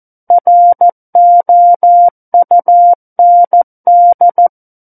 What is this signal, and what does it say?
Morse code, heard as a keyed tone, about 14 wpm, 700 Hz. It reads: ROUND
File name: Book One / 05-Chapter 5 - 14 WPM